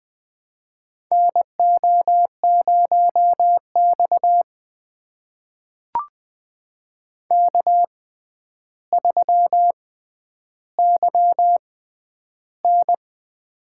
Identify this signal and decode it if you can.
Morse code, heard as a keyed tone, about 20 wpm, spaced as for 10 wpm, 700 Hz. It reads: NO0X K3YN
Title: USA Call Signs - 4 Characters - 6x Character Spacing - 20wpm